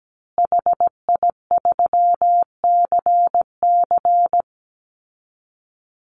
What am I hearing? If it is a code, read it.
Morse code, heard as a keyed tone, about 17 wpm, 700 Hz. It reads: HI3CC
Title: Call Signs Repeated 3x - 17wpm